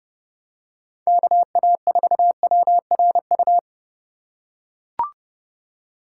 Morse code, heard as a keyed tone, about 32 wpm, 700 Hz. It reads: KA4WRU
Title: USA Call Signs - 6 Characters - 2x Character Spacing - 30wpm